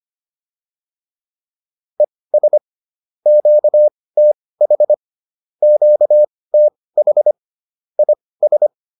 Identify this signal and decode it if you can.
Morse code, heard as a keyed tone, about 25 wpm, spaced as for 18 wpm, 600 Hz. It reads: ES QTH QTH IS